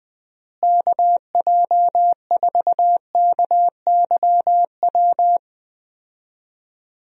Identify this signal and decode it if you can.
Morse code, heard as a keyed tone, about 20 wpm, 700 Hz. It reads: KJ4KYW
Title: USA Call Signs - 6 Characters - Repeated 3x - 20wpm